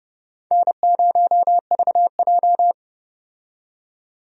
Morse code, heard as a keyed tone, about 30 wpm, 700 Hz. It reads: N0VJ
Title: USA Call Signs - 4 Characters - 30wpm